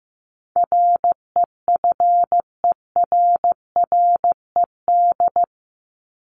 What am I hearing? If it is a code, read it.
Morse code, heard as a keyed tone, about 15 wpm, 700 Hz. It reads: REFERRED
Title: ED Suffix - Top 100 Words with Base Words 15wpm